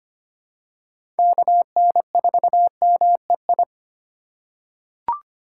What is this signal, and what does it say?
Morse code, heard as a keyed tone, about 25 wpm, 700 Hz. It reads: KN4MEI
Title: USA Call Signs - 2x Character Spacing - 25wpm